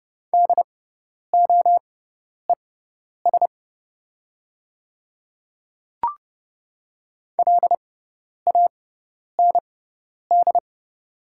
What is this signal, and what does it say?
Morse code, heard as a keyed tone, about 30 wpm, spaced as for 10 wpm, 700 Hz. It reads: DOES LAND
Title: 6x Character Spacing - Top 200 Words - 30wpm